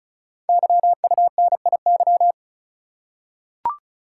Morse code, heard as a keyed tone, about 36 wpm, 690 Hz. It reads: YUNIY
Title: Code Groups - 5 Letters - 35wpm